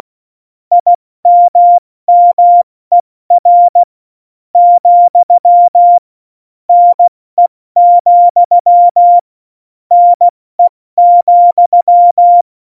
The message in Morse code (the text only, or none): IMMER, NE, NE,